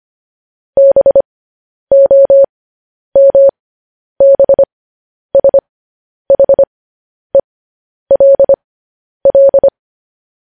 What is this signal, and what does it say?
Morse code, heard as a keyed tone, about 25 wpm, spaced as for 10 wpm, 550 Hz. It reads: BOMBSHELL
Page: News Headlines in Morse Code at 10 WPM